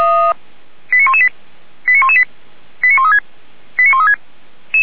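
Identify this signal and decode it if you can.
Five-tone selective call (ZVEI), heard as quick runs of five tones, tones of about 70 ms. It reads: 89118 89118 89136 89136
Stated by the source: ZVEI five-tone call received on SDR.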